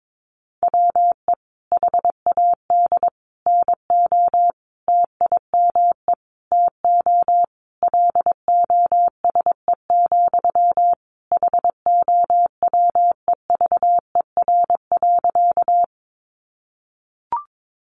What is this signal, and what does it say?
Morse code, heard as a keyed tone, about 22 wpm, 700 Hz. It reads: WE HAD NO TIME TO LOSE, HOWEVER.
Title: Sentences from Top 1100 Words 22wpm